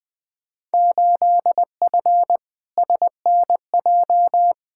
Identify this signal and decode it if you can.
Morse code, heard as a keyed tone, about 20 wpm, 700 Hz. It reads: 8F SNJ
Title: Field Day Exchanges Encore - 20wpm